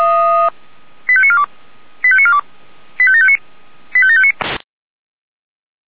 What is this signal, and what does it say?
Five-tone selective call (ZVEI), heard as quick runs of five tones, tones of about 70 ms. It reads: 85942 85942 85759 85759